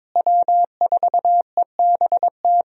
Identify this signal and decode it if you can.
Morse code, heard as a keyed tone, about 22 wpm, 700 Hz. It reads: W4EBT
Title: USA Call Signs - 5 Characters - Repeated 3x - 22wpm